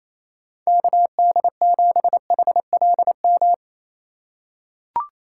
Morse code, heard as a keyed tone, about 28 wpm, 700 Hz. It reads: KD7HLM
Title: USA Call Signs - 6 Characters - 2x Character Spacing - 28wpm